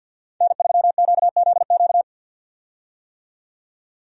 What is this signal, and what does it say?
Morse code, heard as a keyed tone, about 49 wpm, 700 Hz. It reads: N3QZQ
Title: USA Call Signs - 5 Characters - 50wpm